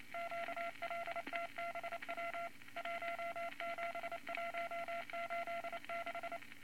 Morse code, heard as a keyed tone, about 28 wpm, 680 Hz, with noise. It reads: QLTBW 17186
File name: Morse Code Records / 22 A LCM 172 Twenty Words Per Minute